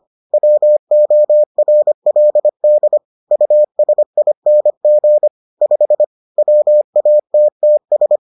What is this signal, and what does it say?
Morse code, heard as a keyed tone, about 25 wpm, 600 Hz. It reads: WORLD USING 5 WATTS